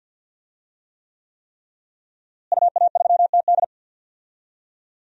Morse code, heard as a keyed tone, about 48 wpm, 700 Hz. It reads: UA3TD
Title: DXCC-100 Call Signs - 5 Characters - Repeated 3x - 50wpm